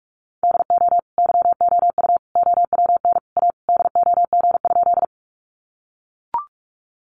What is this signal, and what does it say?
Morse code, heard as a keyed tone, about 44 wpm, 700 Hz. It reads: DO YOU OWN A DOG?
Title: Sentences from Top 400 Words 45wpm